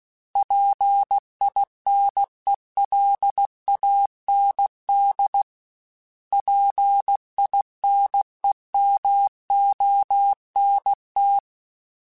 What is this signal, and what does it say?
Morse code, heard as a keyed tone, about 16 wpm, 800 Hz. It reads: PINELAND PINEMONT
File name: fromSMs / FromSMs mp3s / Cities@15